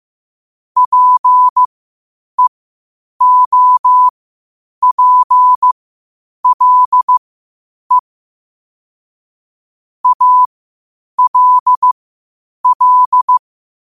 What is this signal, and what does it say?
Morse code, heard as a keyed tone, about 15 wpm, spaced as for 8 wpm, 1000 Hz. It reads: PEOPLE ALL
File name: 8wpm